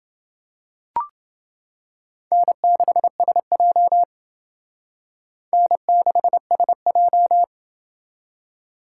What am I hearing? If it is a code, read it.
Morse code, heard as a keyed tone, about 28 wpm, 700 Hz. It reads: N6SJ N6SJ